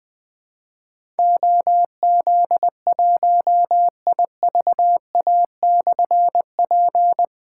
Morse code, heard as a keyed tone, about 20 wpm, 700 Hz. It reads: OZ1IVA/P
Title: Difficult Call Signs 20wpm